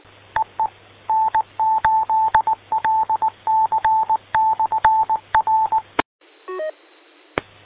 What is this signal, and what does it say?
Morse code, heard as a keyed tone, about 19 wpm, 880 Hz. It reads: EE N8LC/R